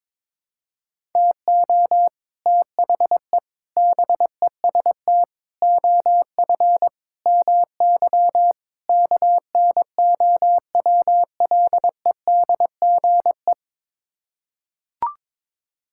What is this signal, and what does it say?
Morse code, heard as a keyed tone, about 22 wpm, 700 Hz. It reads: TO THE BEST OF MY KNOWLEDGE
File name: practice / 22wpm / Sets of 6 Words - 22wpm